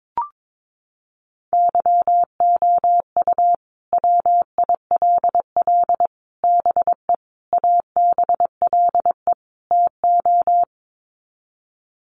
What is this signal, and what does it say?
Morse code, heard as a keyed tone, about 22 wpm, 700 Hz. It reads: YOU WILL BE ABLE TO
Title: Sets of 5 Words Encore2 22wpm